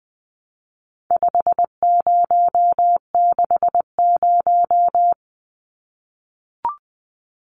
Morse code, heard as a keyed tone, about 20 wpm, 700 Hz. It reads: 5060
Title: Numbers - 4-Digit 20wpm